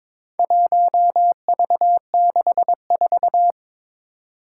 Morse code, heard as a keyed tone, about 22 wpm, 700 Hz. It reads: 1V64
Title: Four Challenging Letter-Number Combinations 22wpm